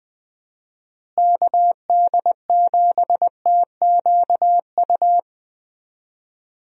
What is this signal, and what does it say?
Morse code, heard as a keyed tone, about 20 wpm, 700 Hz. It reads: KD7TQU
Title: USA Call Signs - 6 Characters - Repeated 3x - 20wpm